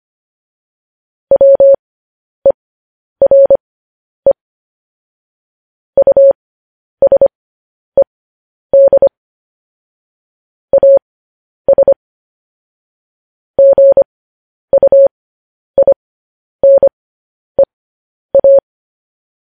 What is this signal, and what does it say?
Morse code, heard as a keyed tone, about 25 wpm, spaced as for 10 wpm, 550 Hz. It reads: WERE USED AS GUINEA
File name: podcast / News headlines-1x-25x10